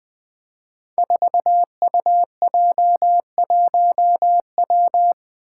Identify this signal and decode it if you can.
Morse code, heard as a keyed tone, about 20 wpm, 700 Hz. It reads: 4UJ1W
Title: Five Challenging Letter-Number Combinations 20wpm